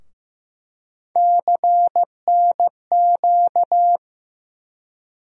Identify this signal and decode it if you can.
Morse code, heard as a keyed tone, about 15 wpm, 700 Hz. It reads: CNQ
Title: USA Call Sign Postfixes - 3 Letters 15wpm